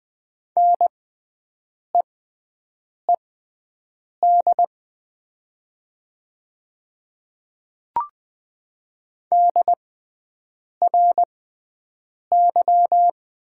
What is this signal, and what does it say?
Morse code, heard as a keyed tone, about 20 wpm, spaced as for 7 wpm, 700 Hz. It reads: NEED DRY
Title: 6x Character Spacing - Top 500 Words - 20wpm